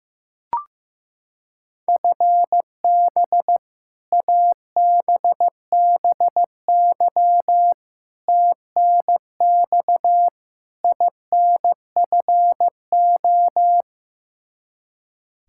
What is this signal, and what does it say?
Morse code, heard as a keyed tone, about 15 wpm, 700 Hz. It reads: FB ABBY TNX INFO
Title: Practice QSO - Exchange 3 - More Info - 15wpm